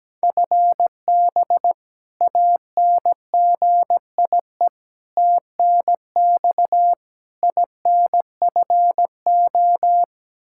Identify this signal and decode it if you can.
Morse code, heard as a keyed tone, about 17 wpm, 700 Hz. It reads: FB ANGIE TNX INFO
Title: K1USN SST QSO Practice 17wpm